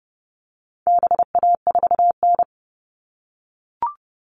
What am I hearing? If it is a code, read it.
Morse code, heard as a keyed tone, about 30 wpm, 700 Hz. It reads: BA4N
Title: All Cut Numbers - 4-Digit 30wpm